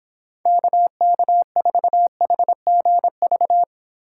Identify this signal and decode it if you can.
Morse code, heard as a keyed tone, about 30 wpm, 700 Hz. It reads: KK4HGV